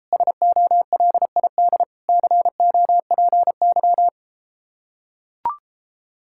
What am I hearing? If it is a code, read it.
Morse code, heard as a keyed tone, about 33 wpm, 700 Hz. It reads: SOLID COPY